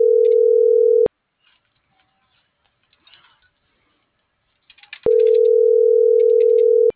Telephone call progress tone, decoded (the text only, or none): ringback tone